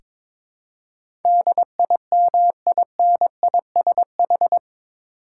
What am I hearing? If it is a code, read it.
Morse code, heard as a keyed tone, about 22 wpm, 700 Hz. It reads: DIMINISH
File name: practice / 22wpm / Eight Letter Words - 22wpm